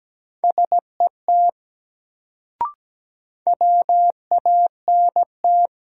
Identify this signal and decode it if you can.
Morse code, heard as a keyed tone, about 17 wpm, 700 Hz. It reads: SET WANT
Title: Sentences from Top 200 Words - New Word Drill - 17wpm